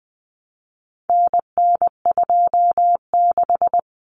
Morse code, heard as a keyed tone, about 20 wpm, 700 Hz. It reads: NN26
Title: Common Cut Numbers - 4-Digit 20wpm